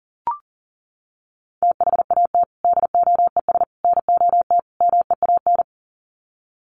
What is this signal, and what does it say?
Morse code, heard as a keyed tone, about 40 wpm, 700 Hz. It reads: THAT DOES NOT MEAN
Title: Sets of 4 Words 40wpm